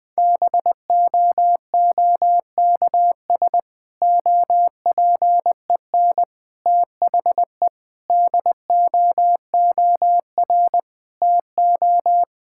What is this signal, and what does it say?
Morse code, heard as a keyed tone, about 21 wpm, 700 Hz. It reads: BOOKS OPEN THE DOOR TO
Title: Difficult Sentences with Hints - 20wpm